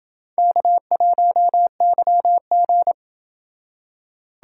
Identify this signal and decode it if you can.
Morse code, heard as a keyed tone, about 26 wpm, 700 Hz. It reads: K1YG